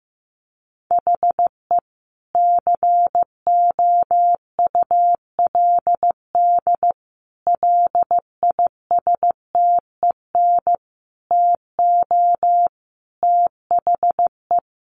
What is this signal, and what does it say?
Morse code, heard as a keyed tone, about 15 wpm, 700 Hz. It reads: HE COULD LISTEN TO THE
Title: Sentences from Top 1500 Words 15wpm